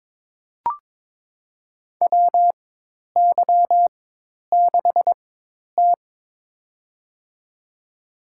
Morse code, heard as a keyed tone, about 22 wpm, spaced as for 10 wpm, 700 Hz. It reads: WY6T
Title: USA Call Signs - 4 Characters - 4x Character Spacing - 22wpm